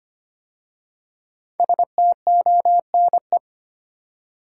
Morse code, heard as a keyed tone, about 25 wpm, 700 Hz. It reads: STONE
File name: 5 Letter Words - First 3 Letters Given - 25wpm